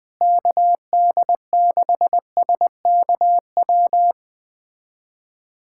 Morse code, heard as a keyed tone, about 20 wpm, 700 Hz. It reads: KD6SKW
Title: USA Callsign DE Callsign - 6 Characters - 20wpm